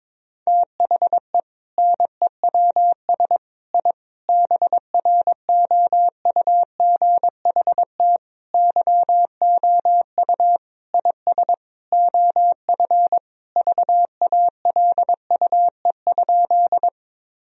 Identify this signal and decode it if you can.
Morse code, heard as a keyed tone, about 22 wpm, 700 Hz. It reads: THE NEWS I BROUGHT YOU IS OF VALUE?
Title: Sentences from Top 1000 Words - Review Part 3 - 22wpm